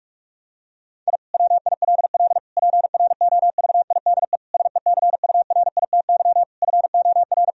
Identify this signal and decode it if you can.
Morse code, heard as a keyed tone, about 41 wpm, 700 Hz. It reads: I WILL PROVIDE SECURITY FOR
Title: Sentences from Top 1000 Words - Review Part 4 - 45wpm